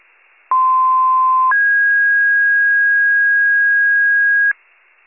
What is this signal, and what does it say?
Two-tone page, heard as a one-second tone, then a three-second tone, about 1040 then 1650 Hz. It is heard from 0.5 to 4.5 s.